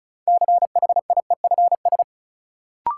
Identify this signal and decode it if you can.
Morse code, heard as a keyed tone, about 36 wpm, 700 Hz. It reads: CHIEFS
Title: Sets of 5 Words Encore - New Word Drill - 35wpm